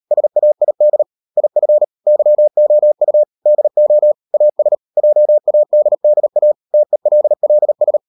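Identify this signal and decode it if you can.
Morse code, heard as a keyed tone, about 38 wpm, 600 Hz. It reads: SAID IF YOU DO AS JADDA TELLS
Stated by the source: Morse code practice at 38 WPM, sending Ken Follett's novel Never in English, from CW med Gnister.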